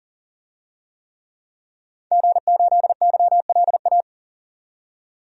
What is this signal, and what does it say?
Morse code, heard as a keyed tone, about 40 wpm, 700 Hz. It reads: G8YLA
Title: DXCC-100 Call Signs - 5 Characters - 40wpm